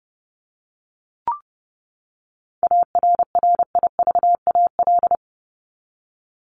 Morse code, heard as a keyed tone, about 30 wpm, 700 Hz. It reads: ARRIVAL
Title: Top 1751-2000 Words 30wpm